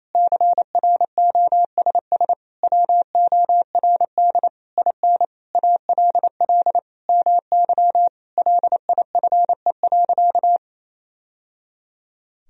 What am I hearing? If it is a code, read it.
Morse code, heard as a keyed tone, about 28 wpm, 700 Hz. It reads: CROSS WORD IN ALL MY LIFE.
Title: Sentences from Top 1000 Words - Review Part 1 - 28wpm